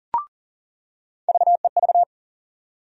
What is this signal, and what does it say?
Morse code, heard as a keyed tone, about 40 wpm, 700 Hz. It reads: VEV
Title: USA Call Sign Postfixes - 3 Letters 40wpm